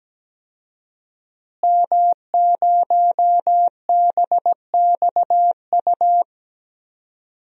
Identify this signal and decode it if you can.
Morse code, heard as a keyed tone, about 17 wpm, 700 Hz. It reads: M0BXU
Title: DXCC-100 Call Signs - 5 Characters - Repeated 3x - 17wpm